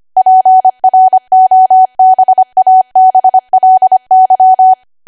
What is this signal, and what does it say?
Morse code, heard as a keyed tone, about 25 wpm, 750 Hz. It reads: PROBABLY
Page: 25 WPM Code Archive